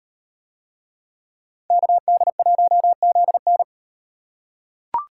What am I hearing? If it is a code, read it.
Morse code, heard as a keyed tone, about 38 wpm, 700 Hz. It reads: KD1ZN